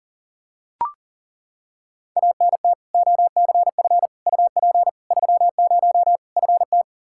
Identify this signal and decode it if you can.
Morse code, heard as a keyed tone, about 40 wpm, 700 Hz. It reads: ANT OCF UP 30 FT
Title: Practice QSO - Exchange 2 - Info - 40wpm